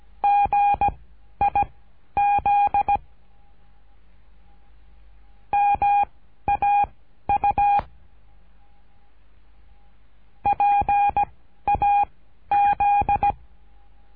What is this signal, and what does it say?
Morse code, heard as a keyed tone, about 17 wpm, spaced as for 11 wpm, 810 Hz. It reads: GIZ MAU PAZ
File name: cursos / CURSO DE TELEGRAFIA / Telegrafia - Trilha 13